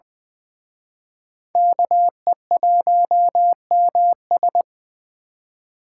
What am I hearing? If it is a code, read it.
Morse code, heard as a keyed tone, about 23 wpm, 700 Hz. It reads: KE1MS
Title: Speed Racing - USA 2x2 Call Signs 20wpm